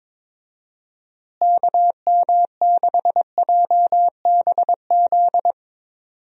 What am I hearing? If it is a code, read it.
Morse code, heard as a keyed tone, about 22 wpm, 700 Hz. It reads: KM6JBZ
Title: USA Call Signs - 6 Characters - Repeated 2x - 22wpm